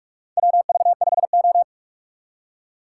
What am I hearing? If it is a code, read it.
Morse code, heard as a keyed tone, about 47 wpm, 700 Hz. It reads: WV5O
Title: USA Call Signs - 4 Characters - 6x Character Spacing - 45wpm